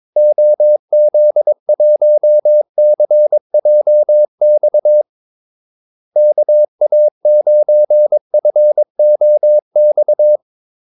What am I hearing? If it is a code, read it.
Morse code, heard as a keyed tone, about 22 wpm, 600 Hz. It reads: OZ1CJX KA9FOX
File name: Callsigns-22wpm